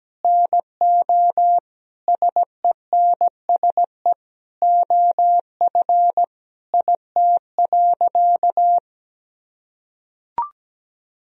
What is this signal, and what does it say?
Morse code, heard as a keyed tone, about 17 wpm, 700 Hz. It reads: NO SENSE OF IT.
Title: Sentences from Top 1000 Words - Review Part 4 - 17wpm